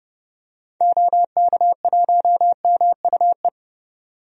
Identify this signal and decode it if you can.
Morse code, heard as a keyed tone, about 30 wpm, 700 Hz. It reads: OK1MUE
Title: DXCC-100 Call Signs - 6 Characters - 30wpm